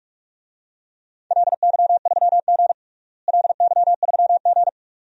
Morse code, heard as a keyed tone, about 43 wpm, 700 Hz. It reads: LY3G LY3G
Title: DXCC-100 Call Signs - 4 Characters - Repeated 2x - 45wpm